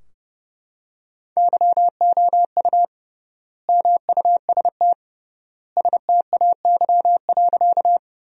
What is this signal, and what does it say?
Morse code, heard as a keyed tone, about 30 wpm, 700 Hz. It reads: YOU MUST STAY.
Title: 3x Word-Spacing - Sentences from Top 500 Words - 30wpm